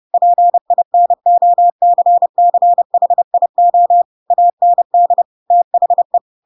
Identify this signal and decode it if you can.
Morse code, heard as a keyed tone, about 30 wpm, 700 Hz. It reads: PINOCCHIO AND THE